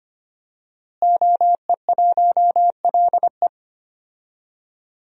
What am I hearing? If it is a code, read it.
Morse code, heard as a keyed tone, about 25 wpm, 700 Hz. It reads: OE1LE
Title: DXCC-100 Call Signs - 5 Characters - Repeated 3x - 25wpm